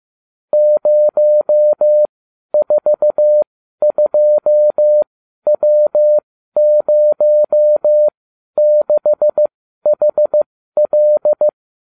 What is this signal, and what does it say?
Morse code, heard as a keyed tone, about 15 wpm, spaced as for 12 wpm, 600 Hz. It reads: 042W06HL